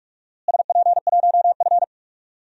Morse code, heard as a keyed tone, about 47 wpm, 700 Hz. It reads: SP1F
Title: DXCC-100 Call Signs - 4 Characters - Repeated 2x - 45wpm